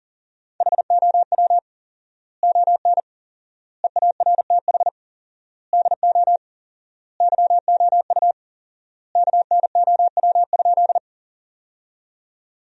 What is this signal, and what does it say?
Morse code, heard as a keyed tone, about 40 wpm, 700 Hz. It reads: HOW ON EARTH DO YOU KNOW?
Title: 4x Word-Spacing - Sentences from Top 200 Words - 40wpm